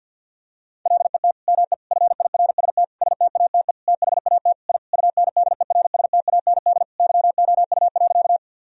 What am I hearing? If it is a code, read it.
Morse code, heard as a keyed tone, about 50 wpm, 700 Hz. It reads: LET ME FIRST STATE THAT I UNDERSTAND YOU.
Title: Sentences from Top 1000 Words - Review Part 2 - 50wpm